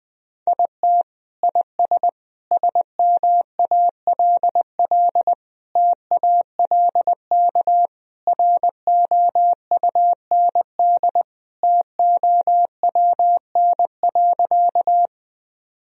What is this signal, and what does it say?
Morse code, heard as a keyed tone, about 20 wpm, 700 Hz. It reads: IT IS SMALL TALK ROUND TOWN.
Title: Sentences from Top 1000 Words - Review Part 1 - 20wpm